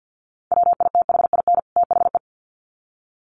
Morse code, heard as a keyed tone, about 48 wpm, 700 Hz. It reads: WITHIN THE
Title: Sets of 2 Words 50wpm